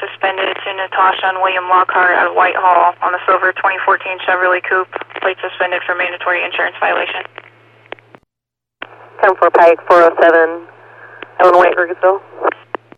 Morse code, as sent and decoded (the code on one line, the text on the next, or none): none